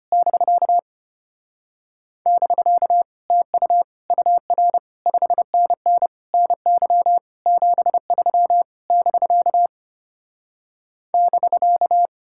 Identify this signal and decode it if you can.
Morse code, heard as a keyed tone, about 33 wpm, 700 Hz. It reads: <BK> <BK> TU UR 5NN NY 73 <BK> <BK>